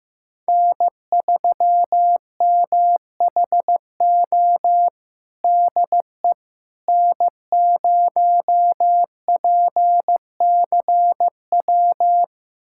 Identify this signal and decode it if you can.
Morse code, heard as a keyed tone, about 15 wpm, 700 Hz. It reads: N3MHO DE N0PCW
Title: USA Callsign DE Callsign - 5 Characters - 15wpm